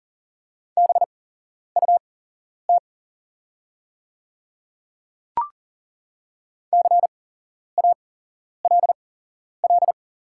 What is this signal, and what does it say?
Morse code, heard as a keyed tone, about 40 wpm, spaced as for 11 wpm, 700 Hz. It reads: BUT CALL
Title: 8x Character Spacing - Top 100 Words - 40wpm